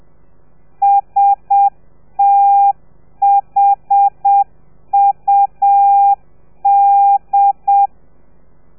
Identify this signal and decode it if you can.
Morse code, heard as a keyed tone, about 7 wpm, 800 Hz. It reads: STHUD